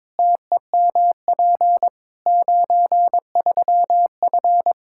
Morse code, heard as a keyed tone, about 22 wpm, 700 Hz. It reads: TEMP 93F
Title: K1USN SST QSO Practice 22wpm